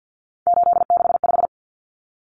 Morse code, heard as a keyed tone, about 50 wpm, 700 Hz. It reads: 865
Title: Common Cut Numbers - 3-Digit 50wpm